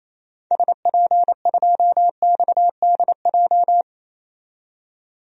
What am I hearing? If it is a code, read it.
Morse code, heard as a keyed tone, about 28 wpm, 700 Hz. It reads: SP2XDJ
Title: DXCC-100 Call Signs - 6 Characters - 28wpm